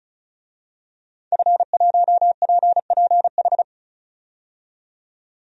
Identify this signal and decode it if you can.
Morse code, heard as a keyed tone, about 36 wpm, 700 Hz. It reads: F1PPH
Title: DXCC-100 Call Signs - 5 Characters - 35wpm